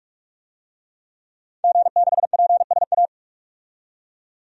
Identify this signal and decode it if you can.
Morse code, heard as a keyed tone, about 41 wpm, 700 Hz. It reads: G6PSA
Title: DXCC-100 Call Signs - 5 Characters - 45wpm